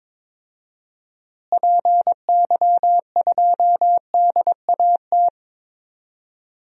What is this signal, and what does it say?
Morse code, heard as a keyed tone, about 22 wpm, 700 Hz. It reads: PY2DAT
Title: DXCC-100 Call Signs - 6 Characters - Repeated 3x - 22wpm